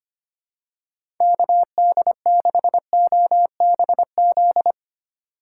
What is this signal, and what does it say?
Morse code, heard as a keyed tone, about 25 wpm, 700 Hz. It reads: KD6OBZ
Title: USA Call Signs - 6 Characters - 2x Character Spacing - 25wpm